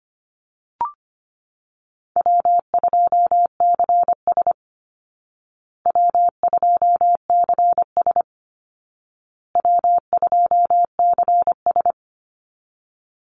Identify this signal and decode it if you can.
Morse code, heard as a keyed tone, about 25 wpm, 700 Hz. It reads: W2CS W2CS W2CS
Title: Easy Call Signs Repeated 3x - 25wpm